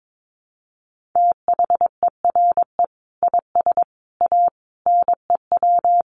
Morse code, heard as a keyed tone, about 22 wpm, 700 Hz. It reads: THERE IS A NEW
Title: Sentences from Top 1500 Words 22wpm